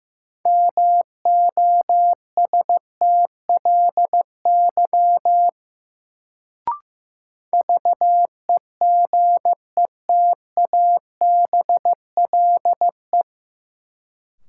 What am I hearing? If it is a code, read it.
Morse code, heard as a keyed tone, about 15 wpm, 700 Hz. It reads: MOSTLY VEGETABLE
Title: Sentences from Top 1300 Words - New Word Drill - 15wpm